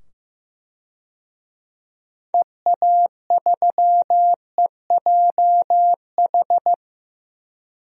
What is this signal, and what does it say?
Morse code, heard as a keyed tone, about 15 wpm, 700 Hz. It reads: EA3EJH
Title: DXCC-100 Call Signs - 6 Characters - 15wpm